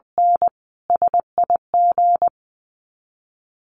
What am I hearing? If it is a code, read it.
Morse code, heard as a keyed tone, about 20 wpm, 700 Hz. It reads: N SIG
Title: Common QSO Phrases 20wpm